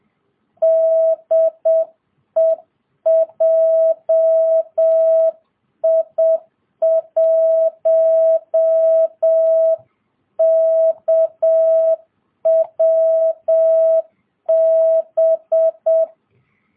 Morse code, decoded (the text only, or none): DE JI1KWB